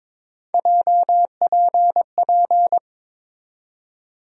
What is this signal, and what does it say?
Morse code, heard as a keyed tone, about 21 wpm, 700 Hz. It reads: JPP
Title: USA Call Sign Postfixes - 3 Letters 22wpm